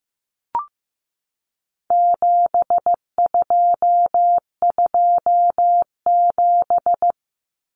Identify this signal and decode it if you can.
Morse code, heard as a keyed tone, about 15 wpm, 700 Hz. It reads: 7227
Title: Common Cut Numbers - 4-Digit 15wpm